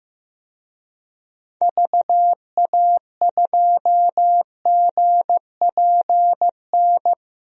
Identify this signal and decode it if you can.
Morse code, heard as a keyed tone, about 15 wpm, 700 Hz. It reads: VA2GPN
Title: DXCC-100 Call Signs - 6 Characters - 15wpm